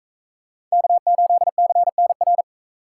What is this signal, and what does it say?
Morse code, heard as a keyed tone, about 41 wpm, 700 Hz. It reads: K8CNR